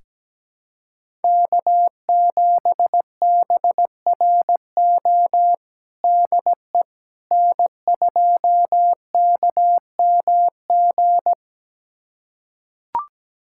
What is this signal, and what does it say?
Morse code, heard as a keyed tone, about 17 wpm, 700 Hz. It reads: K7BRO DE N2KMG
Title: USA Callsign DE Callsign - 5 Characters - 17wpm